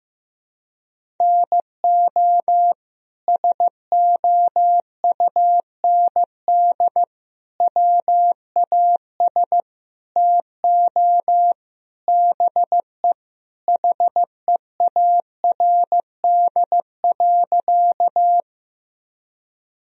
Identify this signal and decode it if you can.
Morse code, heard as a keyed tone, about 15 wpm, 700 Hz. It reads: NO SOUND WAS TO BE HEARD.
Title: Sentences from Top 1000 Words - Review Part 1 - 15wpm